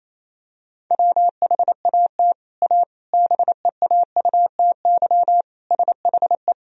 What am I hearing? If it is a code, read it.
Morse code, heard as a keyed tone, about 28 wpm, 700 Hz. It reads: WHAT A BEAUTY SHE
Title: Sentences from Top 1000 Words - Review Part 2 - 28wpm